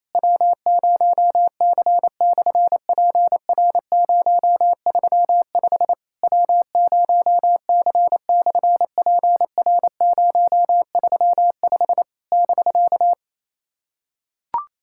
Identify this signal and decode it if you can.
Morse code, heard as a keyed tone, about 27 wpm, 700 Hz. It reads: W0C/PR035 W0C/PR035 <BK>
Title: SOTA QSO Practice 28wpm